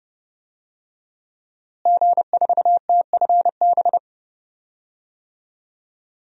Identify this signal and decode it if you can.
Morse code, heard as a keyed tone, about 32 wpm, 700 Hz. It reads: G4TFB
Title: DXCC-100 Call Signs - 5 Characters - Repeated 3x - 30wpm